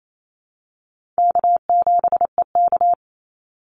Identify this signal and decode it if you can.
Morse code, heard as a keyed tone, about 28 wpm, 700 Hz. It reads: K7EK
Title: Easy Call Signs Repeated 3x - 28wpm